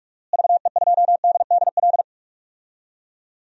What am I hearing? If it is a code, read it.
Morse code, heard as a keyed tone, about 46 wpm, 700 Hz. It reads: VE2DDL
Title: DXCC-100 Call Signs - 6 Characters - Repeated 2x - 45wpm